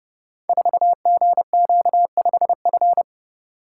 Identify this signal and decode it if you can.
Morse code, heard as a keyed tone, about 30 wpm, 700 Hz. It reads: 4GQ5F
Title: Code Groups - 5 Letter:Numbers - 30wpm